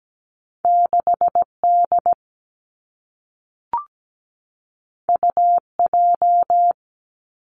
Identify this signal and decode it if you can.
Morse code, heard as a keyed tone, about 17 wpm, 700 Hz. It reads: 6D UJ
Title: Two Challenging Letter-Number Combinations 17wpm